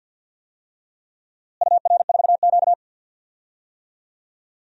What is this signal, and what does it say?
Morse code, heard as a keyed tone, about 49 wpm, 700 Hz. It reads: UR4Q
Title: DXCC-100 Call Signs - 4 Characters - Repeated 2x - 50wpm